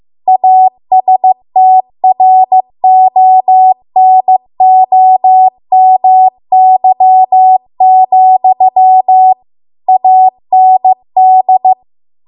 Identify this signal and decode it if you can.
Morse code, heard as a keyed tone, about 15 wpm, 750 Hz. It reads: ASTRONOMY, AND
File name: Archive / 15 WPM / 15WPM